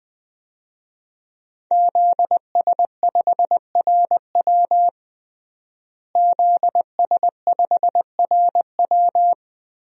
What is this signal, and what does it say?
Morse code, heard as a keyed tone, about 20 wpm, 700 Hz. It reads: ZS5RW ZS5RW
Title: DXCC-100 Call Signs - 5 Characters - Repeated 3x - 20wpm